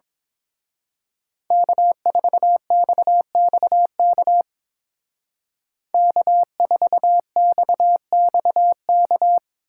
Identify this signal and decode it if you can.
Morse code, heard as a keyed tone, about 26 wpm, 700 Hz. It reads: K4XXK K4XXK
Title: Speed Racing - USA 1x3 Call Signs 22wpm